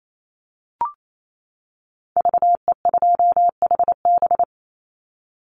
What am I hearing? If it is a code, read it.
Morse code, heard as a keyed tone, about 28 wpm, 700 Hz. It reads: VE2HB
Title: Call Signs - 28wpm